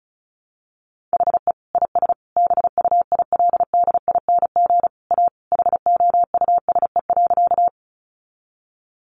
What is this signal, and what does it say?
Morse code, heard as a keyed tone, about 34 wpm, 700 Hz. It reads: HE IS BUILDING A HOUSE.
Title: Sentences from Top 700 Words 35wpm